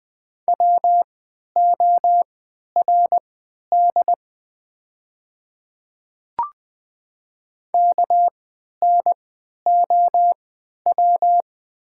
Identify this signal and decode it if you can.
Morse code, heard as a keyed tone, about 20 wpm, spaced as for 11 wpm, 700 Hz. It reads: WORD KNOW
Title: 3x Character Spacing - Top 200 Words - 20wpm